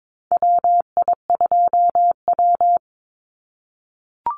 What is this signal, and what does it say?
Morse code, heard as a keyed tone, about 22 wpm, 700 Hz. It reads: WI2W
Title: NA SOTA Chaser Callsigns 22wpm